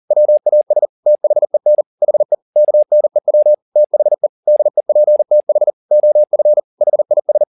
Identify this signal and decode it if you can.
Morse code, heard as a keyed tone, about 40 wpm, 600 Hz. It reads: WAS THEN HE KNEW THE DEPTH OF HIS
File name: the old man and the sea 2 40wpm